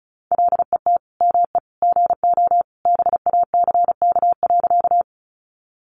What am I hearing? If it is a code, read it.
Morse code, heard as a keyed tone, about 34 wpm, 700 Hz. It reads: LET ME GO BACK.
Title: Sentences from Top 300 Words 35wpm